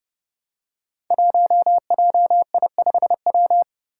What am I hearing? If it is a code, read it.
Morse code, heard as a keyed tone, about 29 wpm, 700 Hz. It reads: 1JI5W
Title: Five Challenging Letter-Number Combinations 30wpm